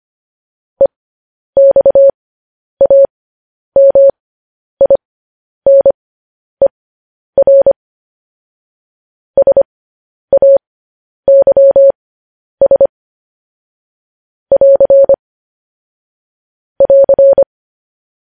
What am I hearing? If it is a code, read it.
Morse code, heard as a keyed tone, about 25 wpm, spaced as for 10 wpm, 550 Hz. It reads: EXAMINER SAYS <AR> <AR>
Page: News Headlines in Morse Code at 10 WPM